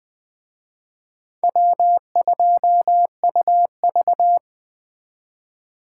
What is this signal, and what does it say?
Morse code, heard as a keyed tone, about 20 wpm, 700 Hz. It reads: W2UV
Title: Speed Racing - USA 1x2 Call Signs 17wpm